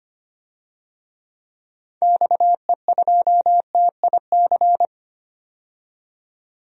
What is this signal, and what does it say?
Morse code, heard as a keyed tone, about 25 wpm, 700 Hz. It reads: XE2TIC
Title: DXCC-100 Call Signs - 6 Characters - 25wpm